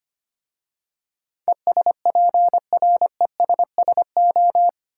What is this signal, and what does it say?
Morse code, heard as a keyed tone, about 25 wpm, 690 Hz. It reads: ESPRESSO
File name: practice / 25wpm / Easy Words with Hints - 25wpm